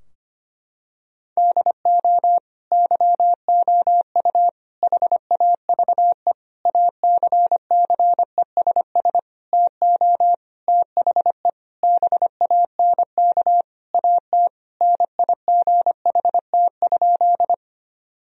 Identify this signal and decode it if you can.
Morse code, heard as a keyed tone, about 25 wpm, 700 Hz. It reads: DO YOU HAVE ACCESS TO THE BANK AT NIGHT?
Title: Sentences from Top 1000 Words - Review Part 4 - 25wpm